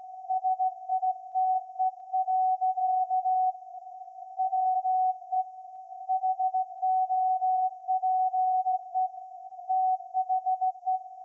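Morse code, heard as a keyed tone, about 17 wpm, 750 Hz. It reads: SITE. WE HOPE THE
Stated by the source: Morse code from SAQ, received at a station in Putignano, Apulia, Italy, on VLF.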